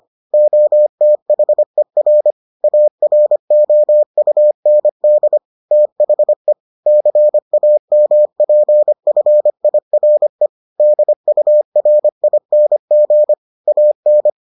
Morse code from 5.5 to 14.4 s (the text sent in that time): THE CAMPFIRE DURING AN